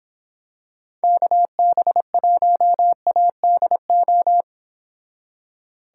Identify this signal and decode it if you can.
Morse code, heard as a keyed tone, about 26 wpm, 700 Hz. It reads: KB1ADO